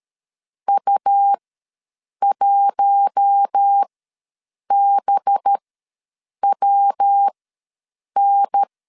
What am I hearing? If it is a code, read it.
Morse code, heard as a keyed tone, about 13 wpm, spaced as for 7 wpm, 800 Hz. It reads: U1BWN
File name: LEZIONI / 5 LBVC / 5 LB ws 12wpm 3034